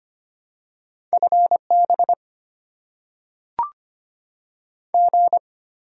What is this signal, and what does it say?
Morse code, heard as a keyed tone, about 25 wpm, 690 Hz. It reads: FB G
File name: Top 20 QSO Elements - 6x Character Spacing - 25wpm